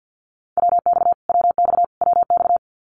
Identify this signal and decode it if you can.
Morse code, heard as a keyed tone, about 50 wpm, 700 Hz. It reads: WX WX WX
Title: Abbreviations Repeated 3x - 50wpm